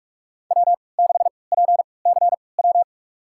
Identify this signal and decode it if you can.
Morse code, heard as a keyed tone, about 46 wpm, spaced as for 26 wpm, 700 Hz. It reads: W6PCW
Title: USA Call Signs - 3x Character Spacing - 45wpm